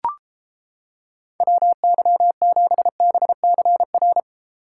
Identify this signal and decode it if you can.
Morse code, heard as a keyed tone, about 31 wpm, 700 Hz. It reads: WY7BCR